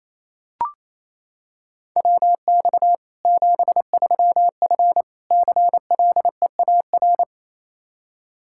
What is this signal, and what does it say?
Morse code, heard as a keyed tone, about 28 wpm, 700 Hz. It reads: WX 73F CLEAR
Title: Practice QSO - Exchange 2 - Info - 28wpm